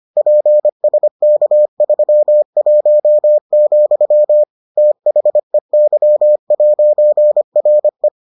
Morse code, heard as a keyed tone, about 25 wpm, 600 Hz. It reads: PSK31, THEY'RE